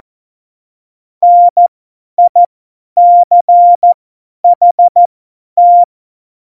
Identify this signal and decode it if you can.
Morse code, heard as a keyed tone, about 14 wpm, spaced as for 10 wpm, 700 Hz. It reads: NICHT
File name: Morsekurs / Witze / Witz 26 Mai 2020, 50 F70